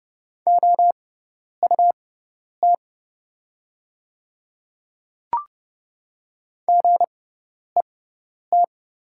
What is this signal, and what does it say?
Morse code, heard as a keyed tone, about 30 wpm, spaced as for 10 wpm, 700 Hz. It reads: OUT GET
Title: 6x Character Spacing - Top 500 Words - 30wpm